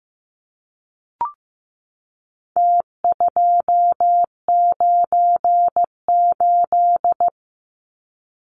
Morse code, heard as a keyed tone, about 15 wpm, 700 Hz. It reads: T298
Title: Numbers - Contest Style - 4-Digit 15wpm